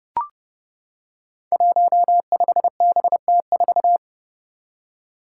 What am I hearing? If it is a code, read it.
Morse code, heard as a keyed tone, about 30 wpm, 700 Hz. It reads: 15BT4
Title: Five Challenging Letter-Number Combinations 30wpm